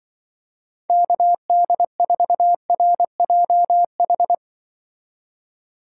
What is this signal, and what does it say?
Morse code, heard as a keyed tone, about 23 wpm, 700 Hz. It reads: KD4RJH
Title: Speed Racing - USA 2x3 Call Signs 20wpm